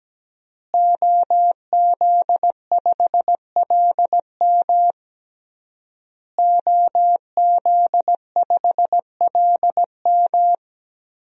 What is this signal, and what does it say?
Morse code, heard as a keyed tone, about 17 wpm, 700 Hz. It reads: OZ5LM OZ5LM
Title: DXCC-100 Call Signs - 5 Characters - Repeated 3x - 17wpm